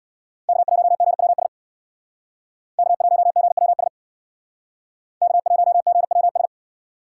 Morse code, heard as a keyed tone, about 61 wpm, 700 Hz. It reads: K1GWS K1GWS K1GWS
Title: Speed Racing - USA 1x3 Call Signs 50wpm